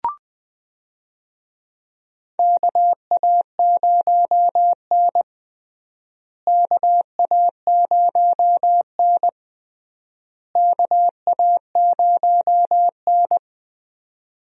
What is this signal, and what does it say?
Morse code, heard as a keyed tone, about 20 wpm, 700 Hz. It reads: KA0N KA0N KA0N
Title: USA Call Signs - 4 Characters - Repeated 3x - 20wpm